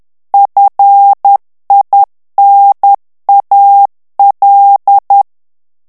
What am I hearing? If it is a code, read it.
Morse code, heard as a keyed tone, about 10 wpm, 800 Hz. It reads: FINAL
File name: ANGLAIS texte clair 3 vitesse 10 mots minute